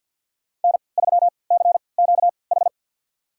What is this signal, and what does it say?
Morse code, heard as a keyed tone, about 51 wpm, spaced as for 28 wpm, 700 Hz. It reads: N3CQH
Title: USA Call Signs - 3x Character Spacing - 50wpm